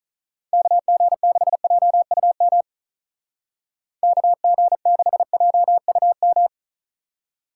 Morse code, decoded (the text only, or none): KG6JUM KG6JUM